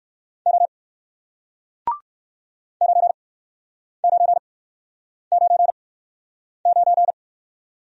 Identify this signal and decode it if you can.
Morse code, heard as a keyed tone, about 59 wpm, spaced as for 9 wpm, 700 Hz. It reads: O 9999